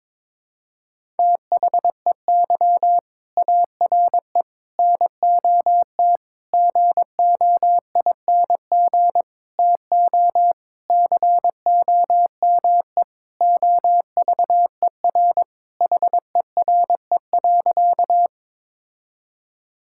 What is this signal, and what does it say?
Morse code, heard as a keyed tone, about 22 wpm, 700 Hz. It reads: THEY ARE NOT GOING TO COME OVER HERE.
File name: practice / 22wpm / Sentences with 4-Word Phrases - 22wpm